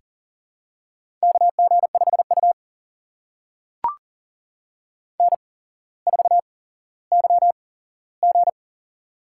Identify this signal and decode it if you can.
Morse code, heard as a keyed tone, about 40 wpm, spaced as for 17 wpm, 700 Hz. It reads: KG5U N4YG
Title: USA Call Signs - 4 Characters - 8x Character Spacing - 40wpm